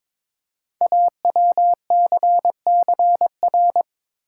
Morse code, heard as a keyed tone, about 22 wpm, 700 Hz. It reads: AWCCR
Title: Code Groups - 5 Letters - 22wpm